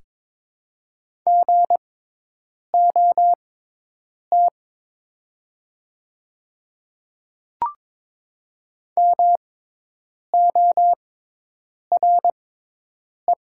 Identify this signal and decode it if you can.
Morse code, heard as a keyed tone, about 22 wpm, spaced as for 8 wpm, 700 Hz. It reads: GOT MORE
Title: 6x Character Spacing - Top 500 Words - 22wpm